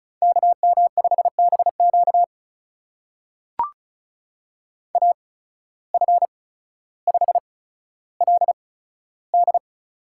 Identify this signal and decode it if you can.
Morse code, heard as a keyed tone, about 36 wpm, spaced as for 15 wpm, 700 Hz. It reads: KM5BQ AF5LD
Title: USA Call Signs - 5 Characters - 8x Character Spacing - 35wpm